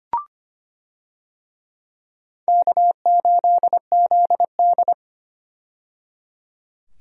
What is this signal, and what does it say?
Morse code, heard as a keyed tone, about 25 wpm, 700 Hz. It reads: K8ZD
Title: USA Call Signs - 4 Characters - 25wpm